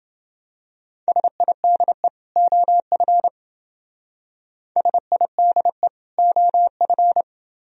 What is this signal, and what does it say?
Morse code, heard as a keyed tone, about 29 wpm, 700 Hz. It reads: SIDE OF SIDE OF